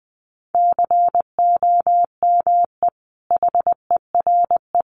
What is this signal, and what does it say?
Morse code, heard as a keyed tone, about 20 wpm, 700 Hz. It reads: COME HERE
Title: Sentences from Top 800 Words 20wpm